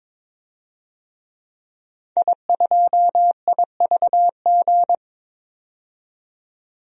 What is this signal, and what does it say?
Morse code, heard as a keyed tone, about 22 wpm, 700 Hz. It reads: I2IVG
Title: DXCC-100 Call Signs - 5 Characters - 22wpm